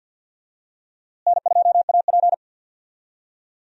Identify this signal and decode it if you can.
Morse code, heard as a keyed tone, about 52 wpm, 700 Hz. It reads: N2AP
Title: USA Call Signs - 4 Characters - Repeated 3x - 50wpm